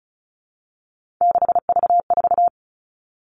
Morse code, heard as a keyed tone, about 34 wpm, 700 Hz. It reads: 6V4
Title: All Cut Numbers - 3-Digit 35wpm